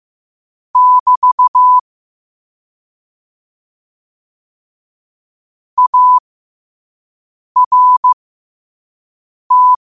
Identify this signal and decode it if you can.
Morse code, heard as a keyed tone, about 15 wpm, spaced as for 5 wpm, 1000 Hz. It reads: <BT> ART
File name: 5wpm